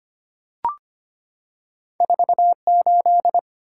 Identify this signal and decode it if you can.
Morse code, heard as a keyed tone, about 25 wpm, 700 Hz. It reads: 4 8
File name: Two Letter-Number Combinations - 25wpm